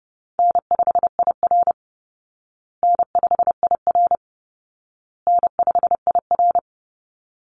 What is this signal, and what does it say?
Morse code, heard as a keyed tone, about 30 wpm, 700 Hz. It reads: N5IR N5IR N5IR
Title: Call Signs Repeated 2x - 30wpm